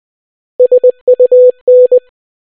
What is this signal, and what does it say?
Morse code, heard as a keyed tone, about 20 wpm, 500 Hz. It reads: SUN